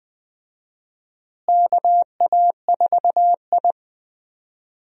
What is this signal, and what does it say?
Morse code, heard as a keyed tone, about 21 wpm, 700 Hz. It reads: KA4I
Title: USA Call Signs - 4 Characters - Repeated 3x - 20wpm